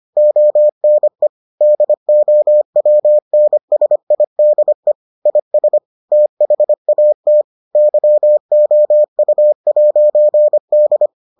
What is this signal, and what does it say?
Morse code, heard as a keyed tone, about 25 wpm, 600 Hz. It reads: ONE DOWNSIDE IS THAT YOU'D